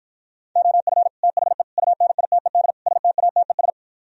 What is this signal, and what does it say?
Morse code, heard as a keyed tone, about 60 wpm, 700 Hz. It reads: OF THE UNITED STATES